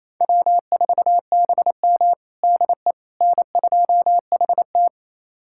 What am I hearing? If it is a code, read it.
Morse code, heard as a keyed tone, about 28 wpm, 700 Hz. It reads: W4BM DE N2HT
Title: USA Callsign DE Callsign - 4 Characters - 28wpm